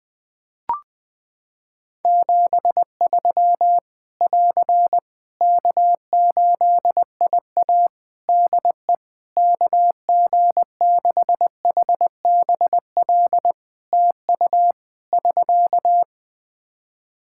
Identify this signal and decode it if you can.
Morse code, heard as a keyed tone, about 20 wpm, 700 Hz. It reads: 73 <AR> K8IA DE KG6HBL TU <SK>
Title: Practice QSO - Exchange 4 - Ending - 20wpm